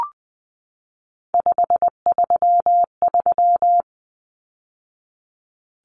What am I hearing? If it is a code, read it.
Morse code, heard as a keyed tone, about 20 wpm, 700 Hz. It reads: 533
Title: Numbers - 3-Digit 20wpm